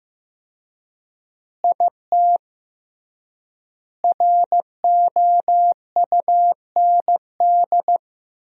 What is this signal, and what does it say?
Morse code, heard as a keyed tone, about 16 wpm, 700 Hz. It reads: IT ROUND
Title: 3x Word-Spacing - Sentences from Top 500 Words - 15wpm